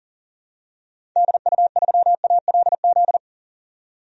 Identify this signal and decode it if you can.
Morse code, heard as a keyed tone, about 40 wpm, 700 Hz. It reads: DU3ALZ
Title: DXCC-100 Call Signs - 6 Characters - 40wpm